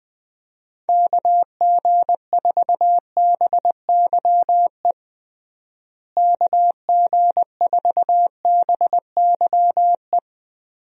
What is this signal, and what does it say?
Morse code, heard as a keyed tone, about 21 wpm, 700 Hz. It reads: KG4BYE KG4BYE
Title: USA Call Signs - 6 Characters - Repeated 3x - 20wpm